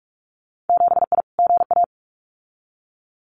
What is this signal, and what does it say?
Morse code, heard as a keyed tone, about 46 wpm, 700 Hz. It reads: 7I GA
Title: Winter Field Day Exchanges Encore 45wpm